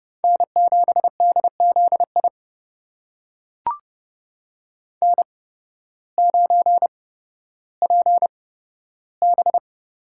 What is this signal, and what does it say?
Morse code, heard as a keyed tone, about 30 wpm, spaced as for 14 wpm, 700 Hz. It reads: N7DZI N9PB
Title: USA Call Signs - 5 Characters - 8x Character Spacing - 30wpm